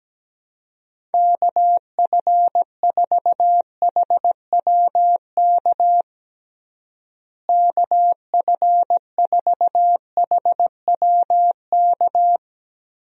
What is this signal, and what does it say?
Morse code, heard as a keyed tone, about 17 wpm, 700 Hz. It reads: KF4HWK KF4HWK
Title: USA Call Signs - 6 Characters - Repeated 2x - 17wpm